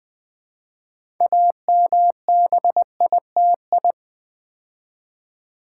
Morse code, heard as a keyed tone, about 20 wpm, 700 Hz. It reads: AMBITI
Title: Build A Word - 3 Letters At A Time - 8-Letter Words - 20wpm